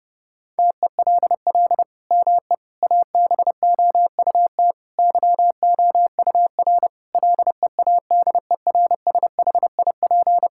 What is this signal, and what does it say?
Morse code, heard as a keyed tone, about 30 wpm, 700 Hz. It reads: TELL ME ABOUT YOUR LEADERSHIP
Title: Sentences from Top 1600 Words 30wpm